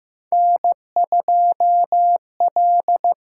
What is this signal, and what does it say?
Morse code, heard as a keyed tone, about 15 wpm, 700 Hz. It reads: N2L
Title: USA Call Signs - 4 Characters - 6x Character Spacing - 15wpm